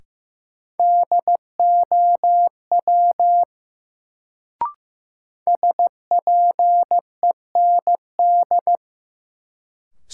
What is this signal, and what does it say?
Morse code, heard as a keyed tone, about 15 wpm, 700 Hz. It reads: DOW SPEND
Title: Sets of 5 Words - New Word Drill - 15wpm